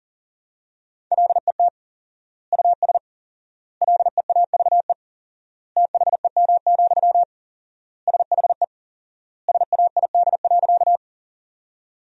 Morse code, heard as a keyed tone, about 40 wpm, 700 Hz. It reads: LET US LEAVE THEM, SHE SAID.
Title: 4x Word-Spacing - Sentences from Top 400 Words - 40wpm